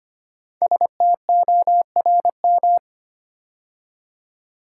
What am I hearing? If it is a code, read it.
Morse code, heard as a keyed tone, about 25 wpm, 700 Hz. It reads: STORM
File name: 5 Letter Words - First 3 Letters Given - 25wpm